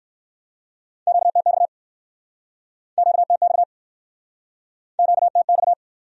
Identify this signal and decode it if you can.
Morse code, heard as a keyed tone, about 64 wpm, 700 Hz. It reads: QTX QTX QTX